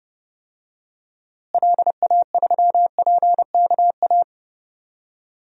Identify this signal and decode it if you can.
Morse code, heard as a keyed tone, about 30 wpm, 700 Hz. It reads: LA3PKA
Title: DXCC-100 Call Signs - 6 Characters - Repeated 2x - 30wpm